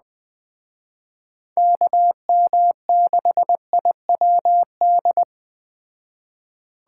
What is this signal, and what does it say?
Morse code, heard as a keyed tone, about 20 wpm, 700 Hz. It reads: KM6IWD
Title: Speed Racing - USA 2x3 Call Signs 17wpm